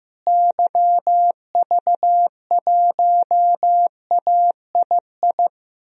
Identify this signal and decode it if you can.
Morse code, heard as a keyed tone, about 15 wpm, 700 Hz. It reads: YV1AII
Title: DXCC-100 Call Signs - 6 Characters - Repeated 2x - 15wpm